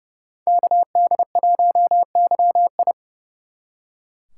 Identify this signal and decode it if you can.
Morse code, heard as a keyed tone, about 36 wpm, 700 Hz. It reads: KD1YI
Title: Speed Racing - USA Call Signs 30wpm